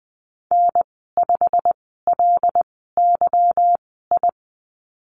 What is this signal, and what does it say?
Morse code, heard as a keyed tone, about 20 wpm, spaced as for 15 wpm, 700 Hz. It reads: N5LYI
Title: 173 Lesson 28g - Number 9 - Call Signs 20wpm